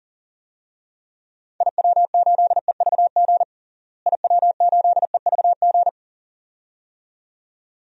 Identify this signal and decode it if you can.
Morse code, heard as a keyed tone, about 40 wpm, 700 Hz. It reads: IW8EVG IW8EVG
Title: DXCC-100 Call Signs - 6 Characters - Repeated 2x - 40wpm